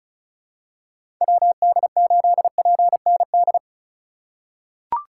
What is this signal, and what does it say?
Morse code, heard as a keyed tone, about 36 wpm, 700 Hz. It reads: WD8PND
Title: USA Call Signs - 6 Characters - 3x Character Spacing - 35wpm